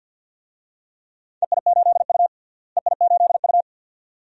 Morse code, heard as a keyed tone, about 53 wpm, 700 Hz. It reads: EI8U EI8U
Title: DXCC-100 Call Signs - 4 Characters - Repeated 2x - 50wpm